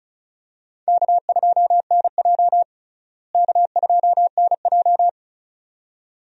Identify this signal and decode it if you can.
Morse code, heard as a keyed tone, about 36 wpm, 700 Hz. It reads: K2NJ K2NJ
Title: USA Call Signs - 4 Characters - Repeated 2x - 35wpm